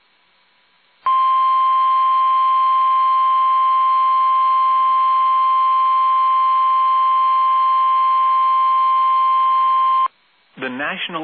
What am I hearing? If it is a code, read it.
Emergency Alert System (SAME) tones (the 1050 Hz attention tone), with speech. It is heard from 1.0 to 10.1 s.